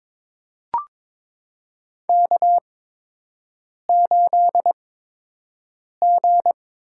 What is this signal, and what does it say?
Morse code, heard as a keyed tone, about 22 wpm, spaced as for 6 wpm, 700 Hz. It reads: K8G
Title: USA Call Signs - 4 Characters - 8x Character Spacing - 22wpm